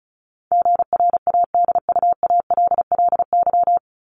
Morse code, heard as a keyed tone, about 34 wpm, 700 Hz. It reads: GRADUALLY
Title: LY Suffix - Top 100 Words with Base Words 35wpm